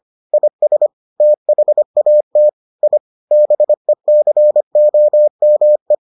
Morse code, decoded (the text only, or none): IS THAT I BECOME